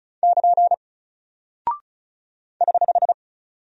Morse code, heard as a keyed tone, about 33 wpm, 700 Hz. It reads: <KN> <HH>
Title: Single Character (Letter:Number:Punctuation:Prosign) Alt 35wpm